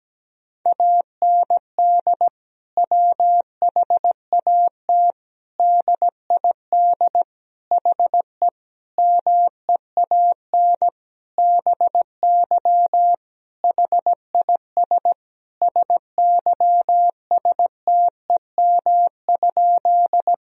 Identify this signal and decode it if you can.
Morse code, heard as a keyed tone, about 17 wpm, 700 Hz. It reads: AND WHAT DID HE MEAN BY HIS SYSTEM?
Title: Sentences from Top 1000 Words - Review Part 3 - 17wpm